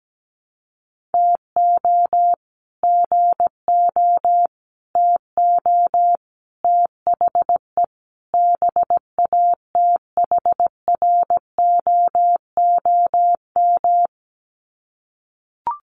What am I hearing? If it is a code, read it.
Morse code, heard as a keyed tone, about 17 wpm, 700 Hz. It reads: TO GO TO THE BATHROOM
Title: Sets of 5 Words Encore 17wpm